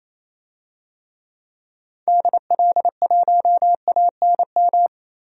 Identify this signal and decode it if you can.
Morse code, heard as a keyed tone, about 28 wpm, 700 Hz. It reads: DL1ANM
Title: DXCC-100 Call Signs - 6 Characters - 28wpm